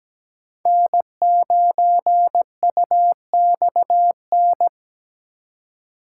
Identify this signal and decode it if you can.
Morse code, heard as a keyed tone, about 17 wpm, 700 Hz. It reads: N9UXN
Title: USA Call Signs - 5 Characters - 4x Character Spacing - 17wpm